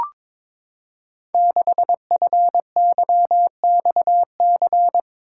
Morse code, heard as a keyed tone, about 22 wpm, 700 Hz. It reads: 6FYXC
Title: Code Groups - 5 Letter:Numbers - 22wpm